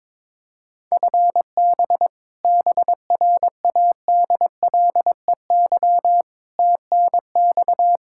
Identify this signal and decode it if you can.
Morse code, heard as a keyed tone, about 22 wpm, 700 Hz. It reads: FB BRADLEY TNX
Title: Practice QSO - Exchange 3 - More Info - 22wpm